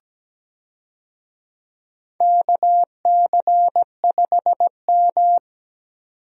Morse code, heard as a keyed tone, about 17 wpm, 700 Hz. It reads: KC5M
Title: USA Call Signs - 4 Characters - 17wpm